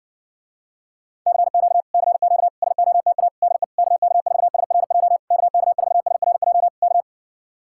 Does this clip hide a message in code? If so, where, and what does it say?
Yes: Morse code from 1.3 to 7.0 s, heard as a keyed tone, about 69 wpm, 700 Hz. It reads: CQ CQ SOTA DE KK4SRJ KK4SRJ K